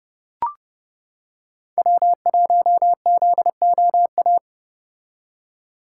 Morse code, heard as a keyed tone, about 30 wpm, 700 Hz. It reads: W1ZOA